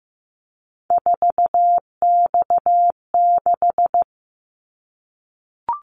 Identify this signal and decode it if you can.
Morse code, heard as a keyed tone, about 15 wpm, 700 Hz. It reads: 4X6
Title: Callsign Prefixes for Top 50 DXCC - 15wpm